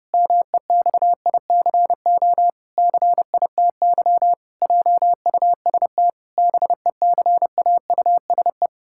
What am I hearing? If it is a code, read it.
Morse code, heard as a keyed tone, about 30 wpm, 700 Hz. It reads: MEXICO CITY JUST BECAUSE
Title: Pangram Challenge v2 - 30wpm